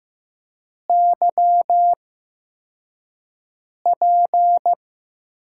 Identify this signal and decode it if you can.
Morse code, heard as a keyed tone, about 15 wpm, spaced as for 4 wpm, 700 Hz. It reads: YP
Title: USA Call Signs - 5 Characters - 8x Character Spacing - 15wpm